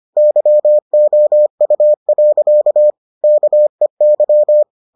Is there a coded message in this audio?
Yes: Morse code from 0.2 to 4.6 s, heard as a keyed tone, about 25 wpm, 600 Hz. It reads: YOU. KEY